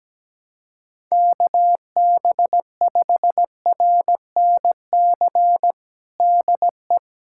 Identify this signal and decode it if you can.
Morse code, heard as a keyed tone, about 17 wpm, 700 Hz. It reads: KB5RNC DE
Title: USA Callsign DE Callsign - 6 Characters - 17wpm